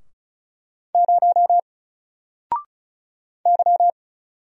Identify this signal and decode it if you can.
Morse code, heard as a keyed tone, about 35 wpm, 700 Hz. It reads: 0 Y
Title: Single Character (Letter:Number:Punctuation:Prosign) 35wpm